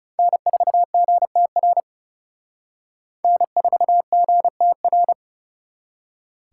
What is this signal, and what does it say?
Morse code, heard as a keyed tone, about 37 wpm, 700 Hz. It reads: N4GTR N4GTR
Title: Speed Racing - USA 1x3 Call Signs 30wpm